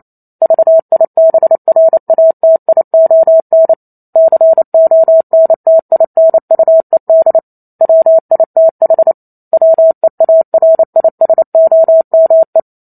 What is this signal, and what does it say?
Morse code, heard as a keyed tone, about 28 wpm, 650 Hz. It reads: VIBRATION CONTINUED WITH WEARISOME